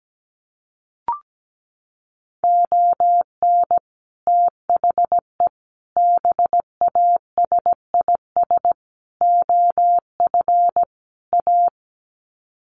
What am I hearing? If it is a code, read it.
Morse code, heard as a keyed tone, about 17 wpm, 700 Hz. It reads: ON THE BASIS OF A
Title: Sets of 5 Words Encore 17wpm